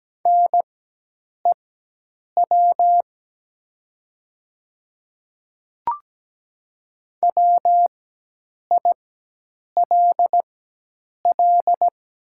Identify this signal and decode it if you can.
Morse code, heard as a keyed tone, about 17 wpm, spaced as for 8 wpm, 700 Hz. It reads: NEW WILL
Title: 4x Character Spacing - Top 200 Words - 17wpm